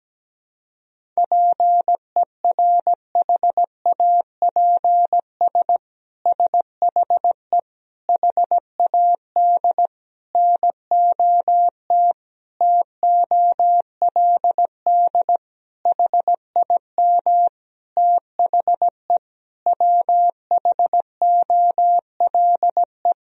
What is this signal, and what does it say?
Morse code, heard as a keyed tone, about 17 wpm, 700 Hz. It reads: PERHAPS SHE HAD NOT TOLD HIM THE WHOLE
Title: Sentences from Top 1000 Words - Review Part 1 - 17wpm